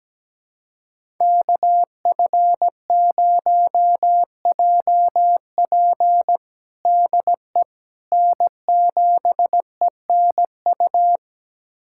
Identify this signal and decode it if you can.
Morse code, heard as a keyed tone, about 17 wpm, 700 Hz. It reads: KF0JP DE N7ENU
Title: USA Callsign DE Callsign - 5 Characters - 17wpm